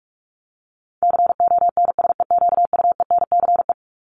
Morse code, heard as a keyed tone, about 44 wpm, 700 Hz. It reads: CONSEQUENCE
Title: LY Suffix - Top 100 Words with Base Words 45wpm